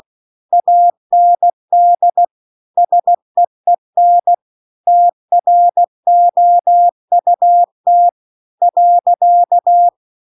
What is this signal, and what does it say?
Morse code, heard as a keyed tone, about 16 wpm, 700 Hz. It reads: AND SEEN TROUT .
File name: CW-The-Big-Twoheaded-River-1-16WPM